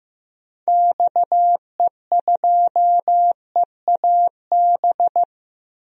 Morse code, heard as a keyed tone, about 16 wpm, 700 Hz. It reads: XE2EAB
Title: DXCC-100 Call Signs - 6 Characters - Repeated 3x - 15wpm